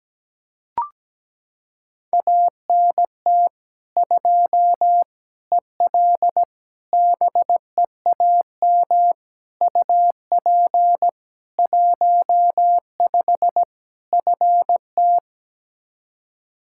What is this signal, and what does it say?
Morse code, heard as a keyed tone, about 17 wpm, 700 Hz. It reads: ANT 2 EL BEAM UP 15 FT
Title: Practice QSO - Exchange 2 - Info - 17wpm